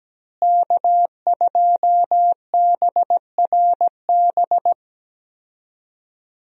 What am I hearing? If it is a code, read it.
Morse code, heard as a keyed tone, about 17 wpm, 700 Hz. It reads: K2BRB
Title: USA Call Signs - 5 Characters - Repeated 2x - 17wpm